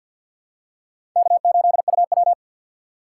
Morse code, heard as a keyed tone, about 48 wpm, 700 Hz. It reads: K8UW
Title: USA Call Signs - 4 Characters - 2x Character Spacing - 50wpm